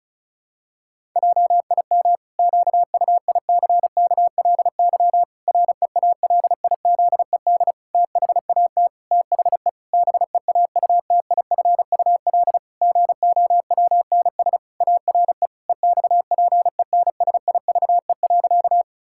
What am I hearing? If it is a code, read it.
Morse code, heard as a keyed tone, about 36 wpm, 700 Hz. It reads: JIM QUICKLY REALIZED THAT THE BEAUTIFUL GOWNS ARE EXPENSIVE.
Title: Pangram Challenge v2 - 35wpm